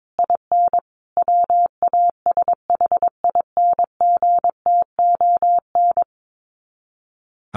Morse code, heard as a keyed tone, about 22 wpm, 700 Hz. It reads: IN WASHINGTON
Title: Sets of 5 Words Encore2 22wpm